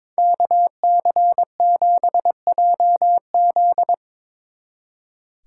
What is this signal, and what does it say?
Morse code, heard as a keyed tone, about 25 wpm, 700 Hz. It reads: KC7JZ
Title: Speed Racing - USA 2x2 Call Signs 22wpm